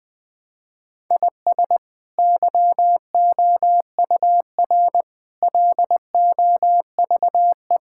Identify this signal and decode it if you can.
Morse code, heard as a keyed tone, about 20 wpm, 700 Hz. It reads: IS YOUR LOVE
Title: Sentences from Top 1000 Words - Review Part 5 - 20wpm